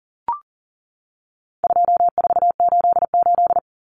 Morse code, heard as a keyed tone, about 40 wpm, 700 Hz. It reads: 2488
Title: Common Cut Numbers - 4-Digit 40wpm